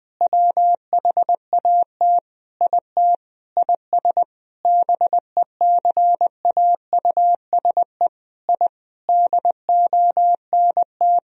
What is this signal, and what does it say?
Morse code, heard as a keyed tone, about 20 wpm, 700 Hz. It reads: WHAT IT IS BECAUSE I DONT
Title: Sentences from Top 1000 Words - Review Part 3 - 20wpm